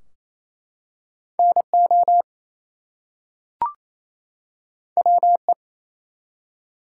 Morse code, heard as a keyed tone, about 27 wpm, 700 Hz. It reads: NO WE
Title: Two Letter Words - 28wpm